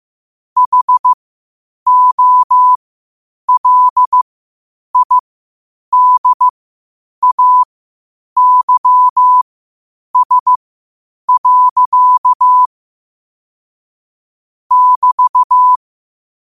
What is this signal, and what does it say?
Morse code, heard as a keyed tone, about 15 wpm, spaced as for 8 wpm, 1000 Hz. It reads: HOLIDAYS. <BT>
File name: qotdfiles / 8wpm